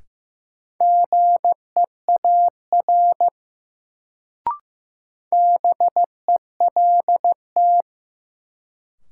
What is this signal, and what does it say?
Morse code, heard as a keyed tone, about 15 wpm, 700 Hz. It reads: GEAR BELT
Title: Sentences from Top 1400 Words - New Word Drill Alt - 15wpm